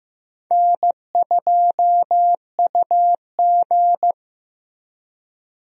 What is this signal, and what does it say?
Morse code, heard as a keyed tone, about 16 wpm, 700 Hz. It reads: N2UG
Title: USA Call Signs - 4 Characters - Repeated 3x - 15wpm